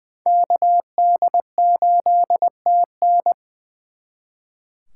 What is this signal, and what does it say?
Morse code, heard as a keyed tone, about 25 wpm, 700 Hz. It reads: KD8TN
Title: Speed Racing - USA Call Signs 20wpm